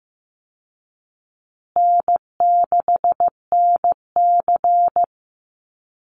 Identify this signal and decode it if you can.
Morse code, heard as a keyed tone, about 15 wpm, 700 Hz. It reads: N6NC
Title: Easy Call Signs Repeated 3x - 15wpm